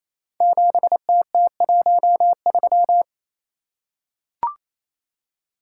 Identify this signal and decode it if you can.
Morse code, heard as a keyed tone, about 29 wpm, 700 Hz. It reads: 7TT13
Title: Code Groups - 5 Letter:Numbers - 28wpm